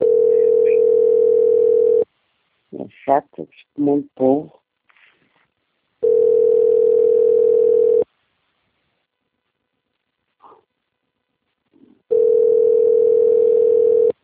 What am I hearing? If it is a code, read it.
Ringback tone (North America), 440 + 480 Hz, with speech.
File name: out- - - - - .4056